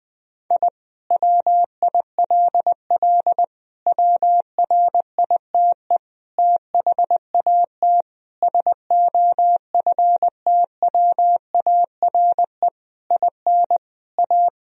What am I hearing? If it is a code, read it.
Morse code, heard as a keyed tone, about 20 wpm, 700 Hz. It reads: I WILL WRITE THAT SOFTWARE IN A
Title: Sentences from Top 1000 Words - Review Part 3 - 20wpm